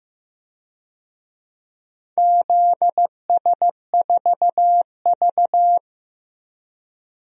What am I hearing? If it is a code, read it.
Morse code, heard as a keyed tone, about 16 wpm, 700 Hz. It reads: ZS4V
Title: DXCC-100 Call Signs - 4 Characters - 15wpm